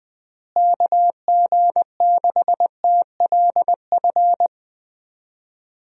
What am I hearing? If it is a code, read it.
Morse code, heard as a keyed tone, about 21 wpm, 700 Hz. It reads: KG6TLF
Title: USA Call Signs - 6 Characters - Repeated 3x - 20wpm